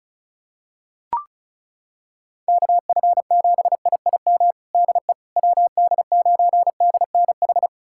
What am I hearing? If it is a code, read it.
Morse code, heard as a keyed tone, about 34 wpm, 700 Hz. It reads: KF7IIM DE WD9DNH
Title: USA Callsign DE Callsign - 6 Characters - 35wpm